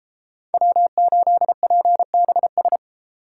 Morse code, heard as a keyed tone, about 33 wpm, 700 Hz. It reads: W8PBS